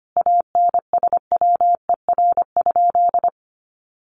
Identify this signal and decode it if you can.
Morse code, heard as a keyed tone, about 25 wpm, 700 Hz. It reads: ANSWER?
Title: Sentences from Top 1300 Words 25wpm